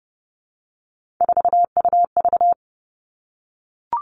Morse code, heard as a keyed tone, about 30 wpm, 700 Hz. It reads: 4UV
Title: Three Challenging Letter-Number Combinations 30wpm